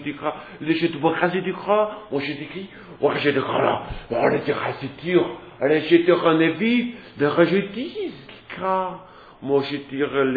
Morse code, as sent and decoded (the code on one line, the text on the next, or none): none